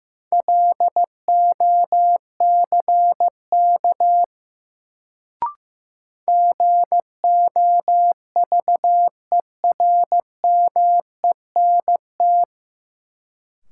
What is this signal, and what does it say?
Morse code, heard as a keyed tone, about 15 wpm, 700 Hz. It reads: LOCK GOVERMENT
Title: Sentences from Top 1300 Words - New Word Drill - 15wpm